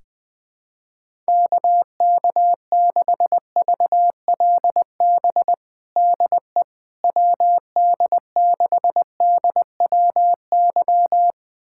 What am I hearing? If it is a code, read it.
Morse code, heard as a keyed tone, about 20 wpm, 700 Hz. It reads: KK6VLB DE WD6DWY
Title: USA Callsign DE Callsign - 6 Characters - 20wpm